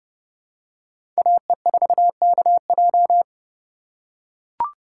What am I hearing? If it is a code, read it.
Morse code, heard as a keyed tone, about 31 wpm, 700 Hz. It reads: AE4KJ
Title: USA Call Signs - 5 Characters - 3x Character Spacing - 30wpm